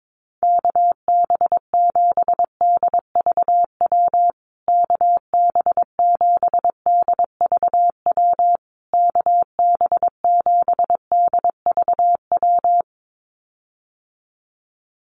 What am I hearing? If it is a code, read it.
Morse code, heard as a keyed tone, about 22 wpm, 700 Hz. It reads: KB7DVW KB7DVW KB7DVW
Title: USA Call Signs Repeated 3x 22wpm